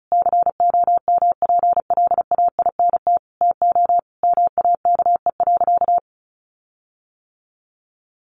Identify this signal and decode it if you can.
Morse code, heard as a keyed tone, about 34 wpm, 700 Hz. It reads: COMPLAINT TO MAKE.
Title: Sentences from Top 1400 Words 35wpm